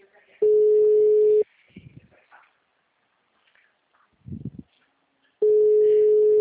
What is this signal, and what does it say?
Ringback tone (Europe, 425 Hz).
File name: out- -2005- - - .88